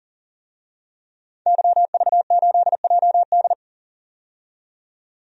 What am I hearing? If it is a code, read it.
Morse code, heard as a keyed tone, about 40 wpm, 700 Hz. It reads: YV8JD
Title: DXCC-100 Call Signs - 5 Characters - 40wpm